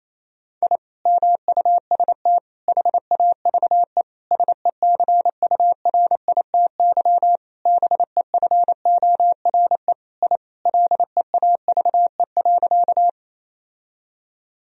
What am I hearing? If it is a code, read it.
Morse code, heard as a keyed tone, about 27 wpm, 700 Hz. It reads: I MUST HAVE SECURITY BEFORE I LEAVE.
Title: Sentences from Top 1000 Words - Review Part 4 - 28wpm